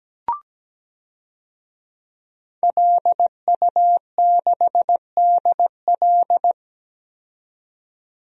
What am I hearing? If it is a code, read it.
Morse code, heard as a keyed tone, about 17 wpm, 700 Hz. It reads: LU6DL
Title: DXCC-100 Call Signs - 5 Characters - 17wpm